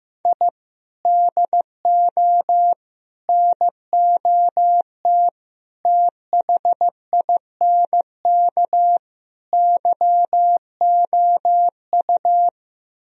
Morse code, heard as a keyed tone, about 16 wpm, 700 Hz. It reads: I DO NOT THINK YOU
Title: Speed-Racing - Sets of 5 Words 15wpm